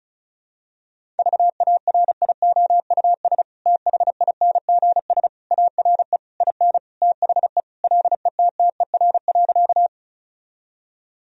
Morse code, heard as a keyed tone, about 34 wpm, 700 Hz. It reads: VARIOUS THINGS ARE IN THE LETTER.
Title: Sentences from Top 1000 Words - Review Part 4 - 35wpm